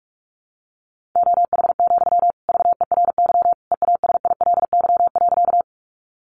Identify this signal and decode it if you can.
Morse code, heard as a keyed tone, about 44 wpm, 700 Hz. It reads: OH, VERY EASILY.
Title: Sentences from Top 900 Words 45wpm